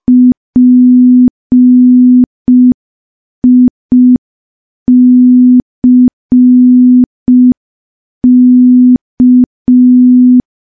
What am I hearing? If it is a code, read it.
Morse code, heard as a keyed tone, about 5 wpm, 250 Hz. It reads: PICK
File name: morse / morse-training-05WPM